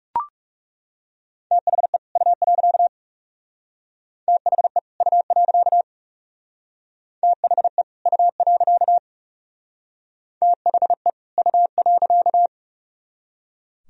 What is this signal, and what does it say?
Morse code, heard as a keyed tone, about 37 wpm, 700 Hz. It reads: THE U. THE U. THE U. THE U.
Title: Speed-Racing - Sets of 2 Words 30wpm